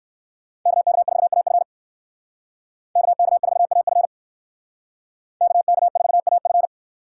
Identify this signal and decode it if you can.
Morse code, heard as a keyed tone, about 58 wpm, 700 Hz. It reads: KK4AF KK4AF KK4AF